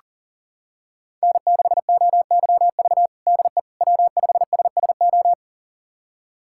Morse code, heard as a keyed tone, about 40 wpm, 700 Hz. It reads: N6OYV DE W5SSO
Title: USA Callsign DE Callsign - 5 Characters - 40wpm